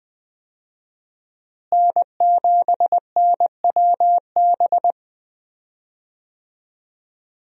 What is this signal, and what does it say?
Morse code, heard as a keyed tone, about 20 wpm, 700 Hz. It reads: N7NWB
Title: USA Call Signs - 5 Characters - Repeated 2x - 20wpm